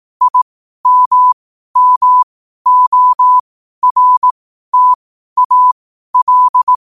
Morse code, heard as a keyed tone, about 18 wpm, spaced as for 12 wpm, 1000 Hz. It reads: IMMORTAL
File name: qotdfiles / 12wpm